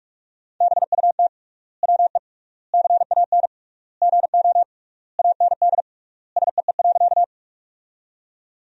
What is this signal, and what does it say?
Morse code, heard as a keyed tone, about 46 wpm, 700 Hz. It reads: BUT WE CAN GO AND SEE.
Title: 3x Word-Spacing - Sentences from Top 100 Words - 45wpm